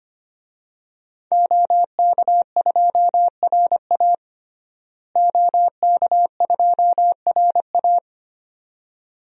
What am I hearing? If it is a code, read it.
Morse code, heard as a keyed tone, about 25 wpm, 700 Hz. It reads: OK2RA OK2RA
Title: DXCC-100 Call Signs - 5 Characters - Repeated 2x - 25wpm